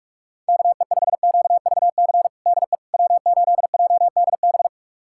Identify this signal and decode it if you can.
Morse code, heard as a keyed tone, about 46 wpm, 700 Hz. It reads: KE5QVC DE W8JDB
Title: USA Callsign DE Callsign 45wpm